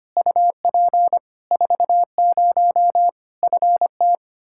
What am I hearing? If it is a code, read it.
Morse code, heard as a keyed tone, about 25 wpm, 700 Hz. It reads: UP 40 FT
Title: K1USN SST QSO Practice 25wpm